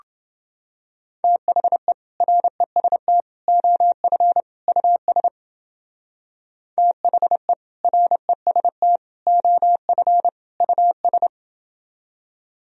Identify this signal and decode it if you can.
Morse code, heard as a keyed tone, about 29 wpm, 700 Hz. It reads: THE REST OF US THE REST OF US